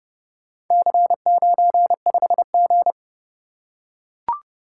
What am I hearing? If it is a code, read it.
Morse code, heard as a keyed tone, about 27 wpm, 700 Hz. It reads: C95G